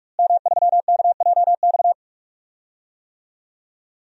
Morse code, heard as a keyed tone, about 48 wpm, 700 Hz. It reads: M3KJX
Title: DXCC-100 Call Signs - 5 Characters - Repeated 3x - 45wpm